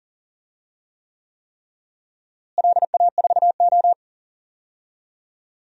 Morse code, heard as a keyed tone, about 40 wpm, 700 Hz. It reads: LA4O
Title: DXCC-100 Call Signs - 4 Characters - Repeated 3x - 40wpm